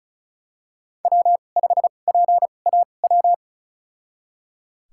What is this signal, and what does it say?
Morse code, heard as a keyed tone, about 34 wpm, spaced as for 25 wpm, 700 Hz. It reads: W5PAW
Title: USA Call Signs - 2x Character Spacing - 35wpm